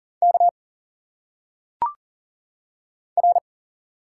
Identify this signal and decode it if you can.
Morse code, heard as a keyed tone, about 40 wpm, 700 Hz. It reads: K R
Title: Single Letters - Phonetic Pronunciation 40wpm